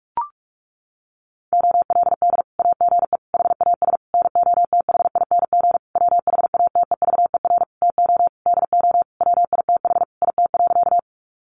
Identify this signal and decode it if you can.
Morse code, heard as a keyed tone, about 44 wpm, 700 Hz. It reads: OLD AGE HAS NOTHING WHATEVER TO DO WITH IT.
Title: Sentences from Top 1100 Words 45wpm